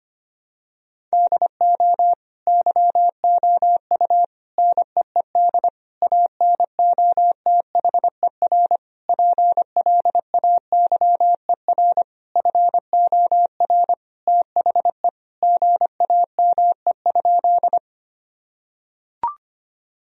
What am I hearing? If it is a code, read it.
Morse code, heard as a keyed tone, about 25 wpm, 700 Hz. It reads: DO YOU NEED ANOTHER PLAYER FOR THE GAME?
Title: Sentences from Top 1000 Words - Review Part 4 - 25wpm